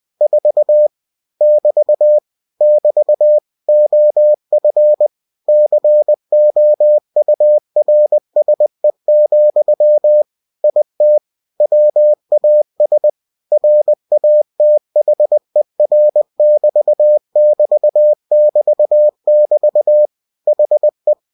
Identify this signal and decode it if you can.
Morse code, heard as a keyed tone, about 20 wpm, 600 Hz. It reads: 4 <BT> <BT> OF COURSE, IT WAS RATHER<BT><BT><BT><BT> HE
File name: cw-books / winnie-the-pooh / winnie-20wpm-page-0004